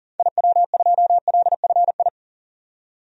Morse code, heard as a keyed tone, about 40 wpm, 700 Hz. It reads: IW2LFI
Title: DXCC-100 Call Signs - 6 Characters - 40wpm